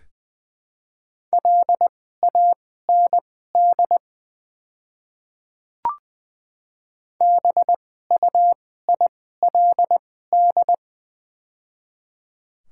Morse code, heard as a keyed tone, about 20 wpm, spaced as for 14 wpm, 700 Hz. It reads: LAND BUILD
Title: 2x Character Spacing - Top 300 Words - 20wpm